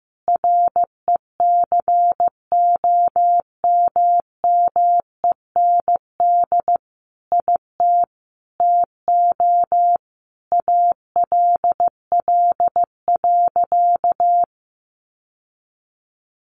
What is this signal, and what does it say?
Morse code, heard as a keyed tone, about 15 wpm, 700 Hz. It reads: RECOMMEND IT TO ALL.
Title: Sentences from Top 1200 Words 15wpm